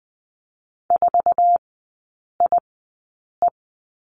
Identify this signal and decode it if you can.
Morse code, heard as a keyed tone, about 20 wpm, spaced as for 8 wpm, 700 Hz. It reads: 4IE
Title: 100 Lesson 17e - Letter u - Three Characters 20wpm